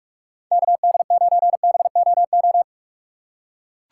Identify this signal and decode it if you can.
Morse code, heard as a keyed tone, about 44 wpm, 700 Hz. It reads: KD9BOO
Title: USA Call Signs - 6 Characters - 6x Character Spacing - 45wpm